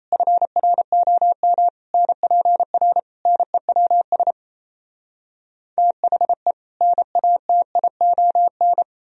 Morse code, heard as a keyed tone, about 37 wpm, 700 Hz. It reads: FROM NPR NEWS THE NATION